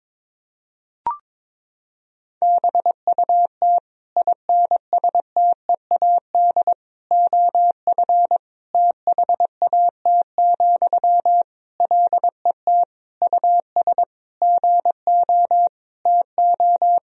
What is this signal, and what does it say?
Morse code, heard as a keyed tone, about 22 wpm, 700 Hz. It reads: BUT INSTEAD OF THAT, LET US GO TO
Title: Sentences from Top 1000 Words - Review Part 4 - 22wpm